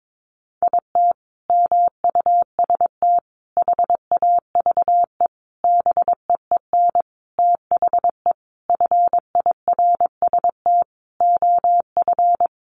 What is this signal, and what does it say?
Morse code, heard as a keyed tone, about 22 wpm, 700 Hz. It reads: IT MUST HAVE BEEN THE FIRST OF
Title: Sentences from Top 800 Words 22wpm